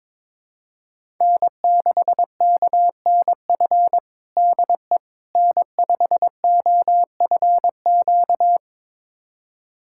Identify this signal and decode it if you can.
Morse code, heard as a keyed tone, about 22 wpm, 700 Hz. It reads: N6KNF DE N5OFQ
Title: USA Callsign DE Callsign - 5 Characters - 22wpm